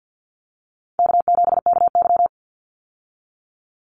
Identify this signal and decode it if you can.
Morse code, heard as a keyed tone, about 50 wpm, 700 Hz. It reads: K7KY
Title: Call Signs - 50wpm